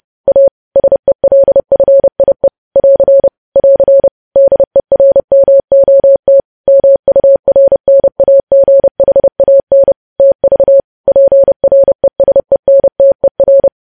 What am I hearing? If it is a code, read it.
Morse code, heard as a keyed tone, about 30 wpm, 550 Hz. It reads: A SELFIE <AR> <AR> DERMOT MURNAGHAN TV PRESENTER
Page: news headlines in morse code at 30 wpm